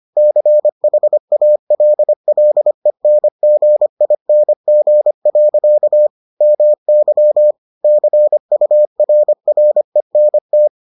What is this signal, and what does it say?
Morse code, heard as a keyed tone, about 25 wpm, 600 Hz. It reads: CHALLENGING. MY CURRENT